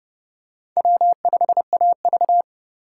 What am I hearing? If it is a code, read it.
Morse code, heard as a keyed tone, about 29 wpm, 700 Hz. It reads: W5AV